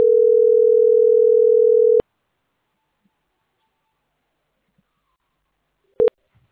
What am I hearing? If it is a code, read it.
Ringback tone (North America), 440 + 480 Hz.